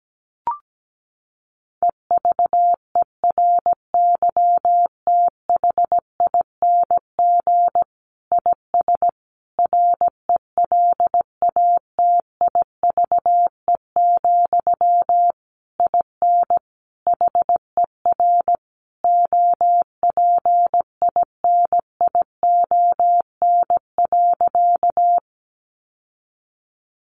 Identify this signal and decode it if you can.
Morse code, heard as a keyed tone, about 17 wpm, 700 Hz. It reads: EVERYTHING IS RELATIVE, IN HER OPINION.
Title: Sentences from Top 1400 Words 17wpm